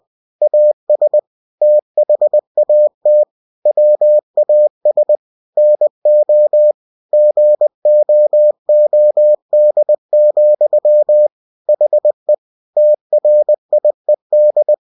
Morse code, heard as a keyed tone, about 20 wpm, 600 Hz. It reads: AS THAT WAS NO GOOD, HE TRIED
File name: cw-books / winnie-the-pooh / winnie-20wpm-page-0039